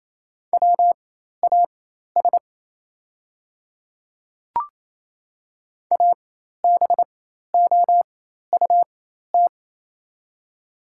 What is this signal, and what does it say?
Morse code, heard as a keyed tone, about 28 wpm, spaced as for 13 wpm, 700 Hz. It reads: WAS ABOUT
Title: 4x Character Spacing - Top 100 Words - 28wpm